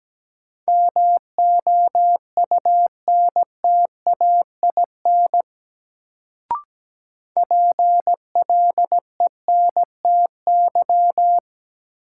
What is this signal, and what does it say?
Morse code, heard as a keyed tone, about 17 wpm, 700 Hz. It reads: MOUNTAIN PLENTY
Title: Sentences from Top 1300 Words - New Word Drill Alt - 17wpm